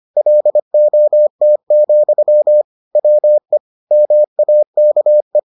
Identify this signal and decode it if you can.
Morse code, heard as a keyed tone, about 25 wpm, 600 Hz. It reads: LOT, WE MAKE